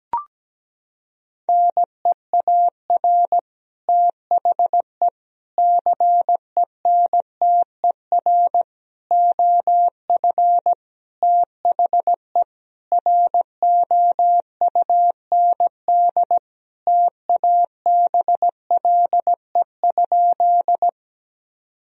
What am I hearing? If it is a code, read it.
Morse code, heard as a keyed tone, about 17 wpm, 700 Hz. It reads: NEAR THE CENTER OF THE ROUND TABLE?
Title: Sentences from Top 1000 Words - Review Part 2 - 17wpm